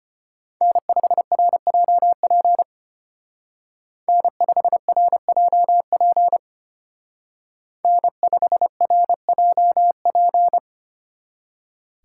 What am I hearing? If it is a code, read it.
Morse code, heard as a keyed tone, about 30 wpm, 700 Hz. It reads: N5RJP N5RJP N5RJP
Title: Speed Racing - USA 1x3 Call Signs 25wpm